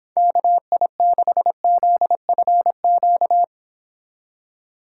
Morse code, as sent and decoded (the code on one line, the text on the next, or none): -.- .. -.... --.. ..-. --.-
KI6ZFQ